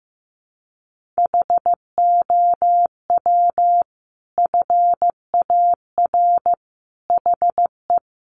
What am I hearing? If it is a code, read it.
Morse code, heard as a keyed tone, about 15 wpm, 700 Hz. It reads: HOW FAR HE
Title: Sentences from Top 1500 Words 15wpm